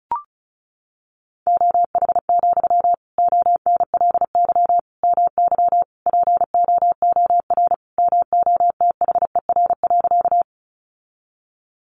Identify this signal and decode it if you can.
Morse code, heard as a keyed tone, about 34 wpm, 700 Hz. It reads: OH, ONLY MY POOR MOTHER.
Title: Sentences from Top 1000 Words 35wpm